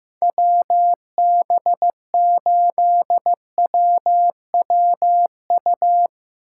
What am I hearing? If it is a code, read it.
Morse code, heard as a keyed tone, about 15 wpm, 700 Hz. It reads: WB8WWU
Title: Speed Racing - USA 2x3 Call Signs 15wpm